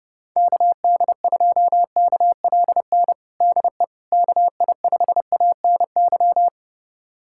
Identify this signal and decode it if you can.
Morse code, heard as a keyed tone, about 30 wpm, 700 Hz. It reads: KD2KLN DE KI5ANY
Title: USA Callsign DE Callsign - 6 Characters - 30wpm